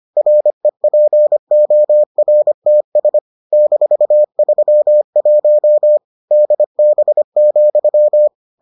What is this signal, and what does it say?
Morse code, heard as a keyed tone, about 25 wpm, 600 Hz. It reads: REPORTS -31 DB,